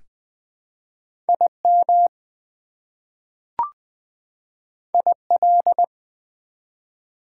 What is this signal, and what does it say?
Morse code, heard as a keyed tone, about 20 wpm, 700 Hz. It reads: IM IL
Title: Prefixes and Suffixes 20wpm